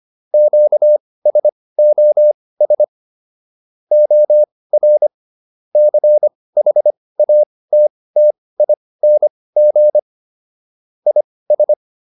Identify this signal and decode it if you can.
Morse code, heard as a keyed tone, about 25 wpm, spaced as for 18 wpm, 600 Hz. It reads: QSOS OR CHATTING IS